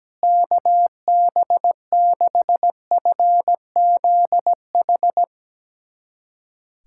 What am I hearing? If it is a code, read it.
Morse code, heard as a keyed tone, about 17 wpm, 700 Hz. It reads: KB6FZH
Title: USA Callsign DE Callsign 17wpm